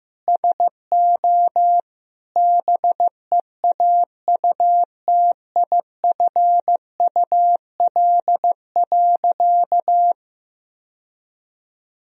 Morse code, heard as a keyed tone, about 15 wpm, 700 Hz. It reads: SO BEAUTIFUL.
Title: Sentences from Top 1000 Words - Review Part 1 - 15wpm